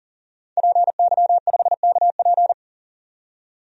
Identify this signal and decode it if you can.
Morse code, heard as a keyed tone, about 38 wpm, 700 Hz. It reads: PY5KP
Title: DXCC-100 Call Signs - 5 Characters - 40wpm